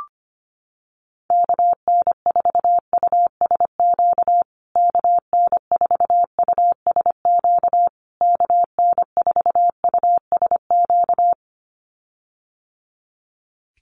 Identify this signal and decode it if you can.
Morse code, heard as a keyed tone, about 25 wpm, 700 Hz. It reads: KN4USQ KN4USQ KN4USQ
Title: USA Call Signs Repeated 3x 25wpm